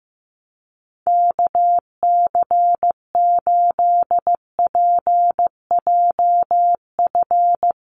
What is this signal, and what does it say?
Morse code, heard as a keyed tone, about 15 wpm, 700 Hz. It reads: KC8PJF
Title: USA Call Signs Repeated 3x 15wpm